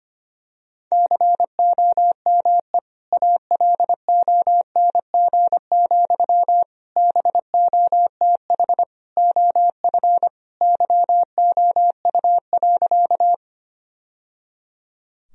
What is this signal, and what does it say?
Morse code, heard as a keyed tone, about 25 wpm, 700 Hz. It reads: COME ALONG, BOTH OF YOU.
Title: Sentences from Top 1000 Words - Review Part 1 - 25wpm